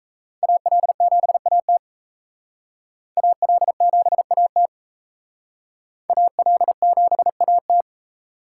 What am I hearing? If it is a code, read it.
Morse code, heard as a keyed tone, about 35 wpm, 700 Hz. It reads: AL7AT AL7AT AL7AT
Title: Speed Racing - USA 2x2 Call Signs 28wpm